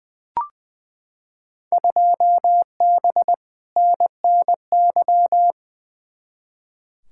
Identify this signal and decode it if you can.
Morse code, heard as a keyed tone, about 21 wpm, 700 Hz. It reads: 2B NNY
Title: Field Day Exchanges Encore - 20wpm